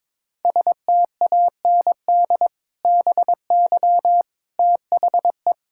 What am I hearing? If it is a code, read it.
Morse code, heard as a keyed tone, about 22 wpm, 700 Hz. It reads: STAND BY THE
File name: practice / 22wpm / Sentences with 2-Word Phrases - 22wpm